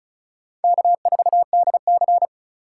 Morse code, heard as a keyed tone, about 34 wpm, 700 Hz. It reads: K4DC
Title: USA Call Signs - 4 Characters - 3x Character Spacing - 35wpm